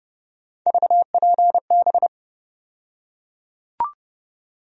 Morse code, heard as a keyed tone, about 30 wpm, 700 Hz. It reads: VPB
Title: USA Call Sign Postfixes - 3 Letters 30wpm